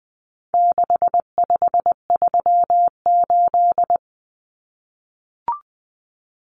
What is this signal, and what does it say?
Morse code, heard as a keyed tone, about 20 wpm, 700 Hz. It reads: 6538
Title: Common Cut Numbers - 4-Digit 20wpm